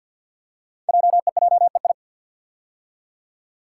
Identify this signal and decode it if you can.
Morse code, heard as a keyed tone, about 51 wpm, 700 Hz. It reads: JEJEI
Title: Code Groups - 5 Letter:Numbers - 50wpm